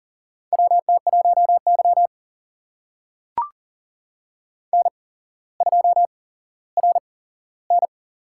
Morse code, heard as a keyed tone, about 40 wpm, spaced as for 17 wpm, 700 Hz. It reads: WT1Y N2RN
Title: USA Call Signs - 4 Characters - 8x Character Spacing - 40wpm